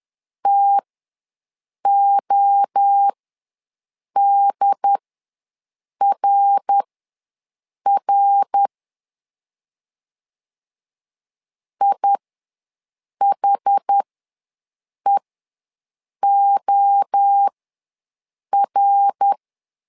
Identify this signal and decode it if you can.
Morse code, heard as a keyed tone, about 10 wpm, spaced as for 6 wpm, 800 Hz. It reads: TODRR IHEOR
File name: LEZIONI / 3 MWRO / 3 MWRO ws 10wpm 2310